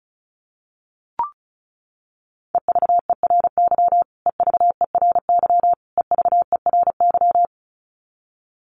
Morse code, heard as a keyed tone, about 34 wpm, 700 Hz. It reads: EVERY EVERY EVERY
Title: Top 500 Words Repeated 35wpm